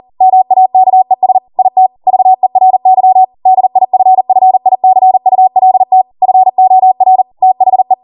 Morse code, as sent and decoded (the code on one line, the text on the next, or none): -- .- -.- . ... / .. - / ...- . .-. -.-- / -.. .. ..-. ..-. .. -.-. ..- .-.. - / ..-. --- .-. / - .... .
MAKES IT VERY DIFFICULT FOR THE